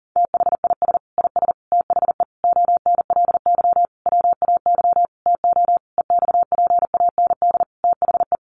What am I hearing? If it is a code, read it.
Morse code, heard as a keyed tone, about 40 wpm, 700 Hz. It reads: THIS IS THE ONLY WAY TO EXPAND THE
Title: Sentences from Top 1400 Words 40wpm